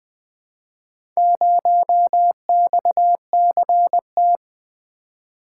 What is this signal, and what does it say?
Morse code, heard as a keyed tone, about 20 wpm, 700 Hz. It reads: 0XCT
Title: Four Letter-Number Combinations - 20wpm